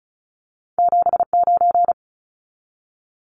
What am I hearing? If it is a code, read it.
Morse code, heard as a keyed tone, about 36 wpm, 700 Hz. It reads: 79
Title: Numbers - 2-Digit 35wpm